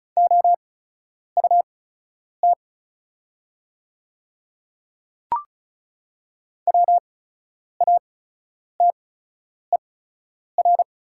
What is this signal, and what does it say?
Morse code, heard as a keyed tone, about 36 wpm, spaced as for 10 wpm, 700 Hz. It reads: OUT WATER
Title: 8x Character Spacing - Top 100 Words - 35wpm